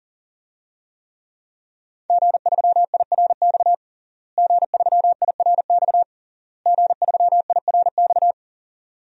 Morse code, heard as a keyed tone, about 40 wpm, 700 Hz. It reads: G3IRX G3IRX G3IRX
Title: DXCC-100 Call Signs - 5 Characters - Repeated 3x - 40wpm